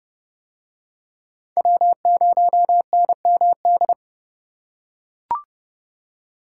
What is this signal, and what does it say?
Morse code, heard as a keyed tone, about 27 wpm, 700 Hz. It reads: W0NMD